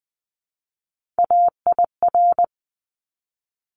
Morse code, heard as a keyed tone, about 20 wpm, 700 Hz. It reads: AIR
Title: Top 200 Words 20wpm